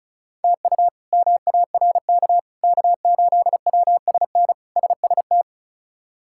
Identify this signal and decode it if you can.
Morse code, heard as a keyed tone, about 34 wpm, 700 Hz. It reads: TU MARK K8WSN SST
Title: K1USN SST QSO Practice - 35wpm